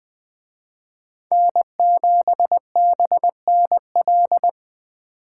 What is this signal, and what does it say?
Morse code, heard as a keyed tone, about 20 wpm, 700 Hz. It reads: N7BNL
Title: Speed Racing - USA 1x3 Call Signs 17wpm